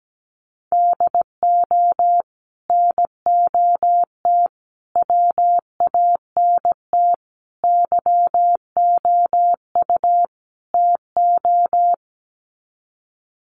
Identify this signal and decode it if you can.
Morse code, heard as a keyed tone, about 17 wpm, 700 Hz. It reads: DO NOT WANT YOU TO
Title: Sets of 5 Words 17wpm